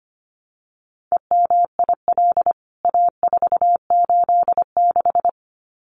Morse code, heard as a keyed tone, about 25 wpm, 700 Hz. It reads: EMIL A486
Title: CWT Exchange 25wpm